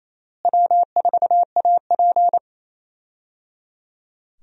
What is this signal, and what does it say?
Morse code, heard as a keyed tone, about 27 wpm, 700 Hz. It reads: W4AP
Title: USA Call Signs - 4 Characters - Repeated 3x - 28wpm